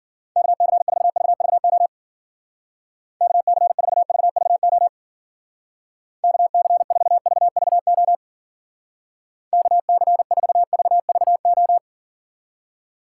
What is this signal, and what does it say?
Morse code, heard as a keyed tone, about 52 wpm, 700 Hz. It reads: KC4VVO KC4VVO KC4VVO KC4VVO